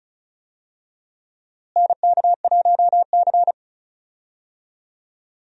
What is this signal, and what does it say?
Morse code, heard as a keyed tone, about 34 wpm, 700 Hz. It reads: NK1C
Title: USA Call Signs - 4 Characters - Repeated 2x - 35wpm